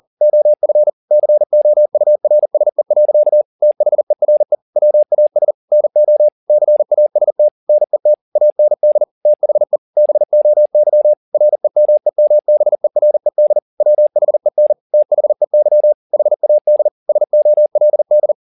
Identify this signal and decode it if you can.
Morse code, heard as a keyed tone, about 40 wpm, 600 Hz. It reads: OF COURSE. THERE WAS NO CAST NET AND THE BOY REMEMBERED WHEN THEY HAD SOLD